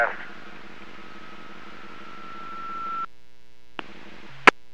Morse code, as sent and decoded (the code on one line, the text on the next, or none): none